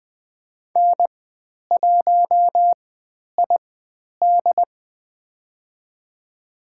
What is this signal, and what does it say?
Morse code, heard as a keyed tone, about 20 wpm, spaced as for 10 wpm, 700 Hz. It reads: N1ID
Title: 109 Lesson 18g - Letter m - Call Signs 20wpm